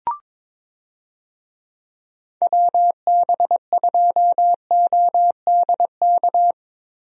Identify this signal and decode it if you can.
Morse code, heard as a keyed tone, about 22 wpm, 700 Hz. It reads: WB2ODK
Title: USA Call Signs - 6 Characters - Repeated 2x - 22wpm